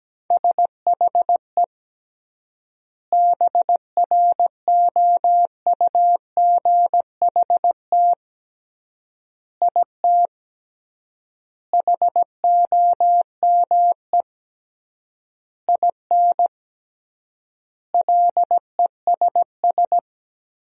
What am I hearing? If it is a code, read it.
Morse code, heard as a keyed tone, about 17 wpm, 700 Hz. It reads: SHE BROUGHT IT HOME IN LESS
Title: 3x Word-Spacing - Sentences from Top 500 Words - 17wpm